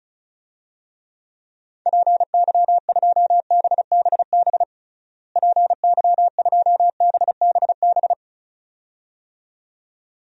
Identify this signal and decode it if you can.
Morse code, heard as a keyed tone, about 36 wpm, 700 Hz. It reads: PY2BBB PY2BBB
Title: DXCC-100 Call Signs - 6 Characters - Repeated 2x - 35wpm